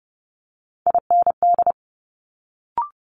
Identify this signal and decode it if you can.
Morse code, heard as a keyed tone, about 30 wpm, 700 Hz. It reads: IND
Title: Multi-letter Phonemes Quick Fire 30wpm